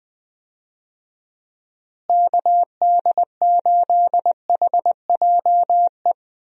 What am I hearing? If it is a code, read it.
Morse code, heard as a keyed tone, about 21 wpm, 700 Hz. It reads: KD8HJE
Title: USA Call Signs - 6 Characters - Repeated 2x - 20wpm